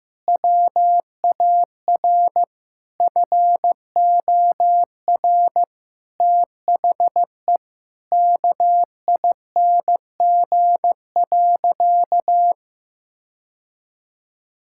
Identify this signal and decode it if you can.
Morse code, heard as a keyed tone, about 16 wpm, 700 Hz. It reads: WAR FOR THE KING.
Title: Sentences from Top 1000 Words - Review Part 6 - 15wpm